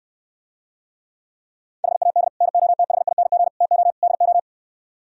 Morse code, heard as a keyed tone, about 73 wpm, 700 Hz. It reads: HAD NOTHING TO DO